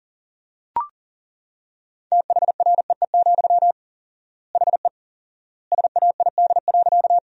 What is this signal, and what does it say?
Morse code, heard as a keyed tone, about 40 wpm, 700 Hz. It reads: THREE, HE SAID.
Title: 4x Word-Spacing - Sentences from Top 200 Words - 40wpm